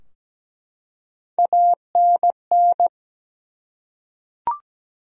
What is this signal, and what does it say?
Morse code, heard as a keyed tone, about 17 wpm, 700 Hz. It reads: ANN
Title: USA Common Names - Max 3 Letters - 17wpm